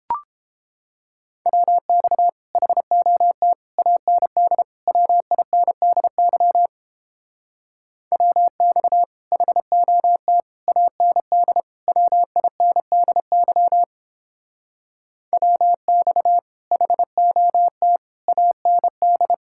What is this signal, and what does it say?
Morse code, heard as a keyed tone, about 29 wpm, 700 Hz. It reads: WX HOT AND WINDY WX HOT AND WINDY WX HOT AND